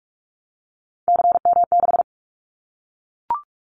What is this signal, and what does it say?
Morse code, heard as a keyed tone, about 44 wpm, 700 Hz. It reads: CM6
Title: Callsign Prefixes for Top 100 DXCC - 45wpm